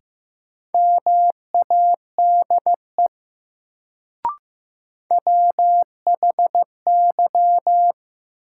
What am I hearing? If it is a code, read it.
Morse code, heard as a keyed tone, about 15 wpm, 700 Hz. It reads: MADE WHY
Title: Sentences from Top 200 Words - New Word Drill - 15wpm